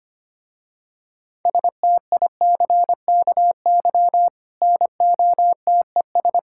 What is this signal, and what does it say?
Morse code, heard as a keyed tone, about 25 wpm, 700 Hz. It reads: STICKY NOTES
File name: practice / 25wpm / Difficult Words with Hints - 25wpm